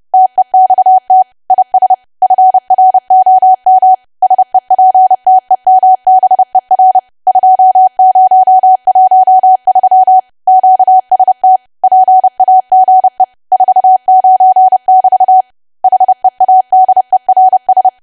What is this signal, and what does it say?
Morse code, heard as a keyed tone, about 30 wpm, 750 Hz. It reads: TEXT IS FROM SEPTEMBER 2013 QST PAGE 49<BT> HEADERS
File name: cwa / W1AW-Level3 / 30WPM